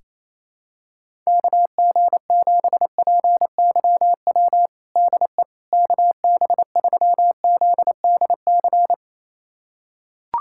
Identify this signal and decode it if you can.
Morse code, heard as a keyed tone, about 28 wpm, 700 Hz. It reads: KG7PYW DE KB3ZDC
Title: USA Callsign DE Callsign - 6 Characters - 28wpm